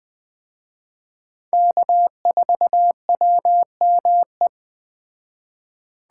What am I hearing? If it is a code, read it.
Morse code, heard as a keyed tone, about 20 wpm, 700 Hz. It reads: K4WME
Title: Speed Racing - USA Call Signs 17wpm